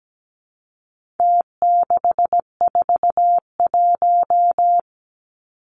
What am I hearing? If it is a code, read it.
Morse code, heard as a keyed tone, about 17 wpm, 700 Hz. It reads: T641
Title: Numbers - Contest Style - 4-Digit 17wpm